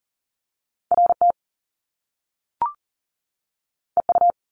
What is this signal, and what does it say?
Morse code, heard as a keyed tone, about 41 wpm, 700 Hz. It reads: RT EU
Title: Top Two Letter Combinations in Words 40wpm